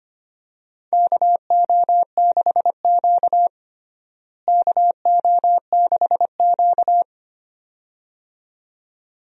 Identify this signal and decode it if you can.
Morse code, heard as a keyed tone, about 25 wpm, 700 Hz. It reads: KO6Q KO6Q
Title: USA Call Signs - 4 Characters - Repeated 2x - 25wpm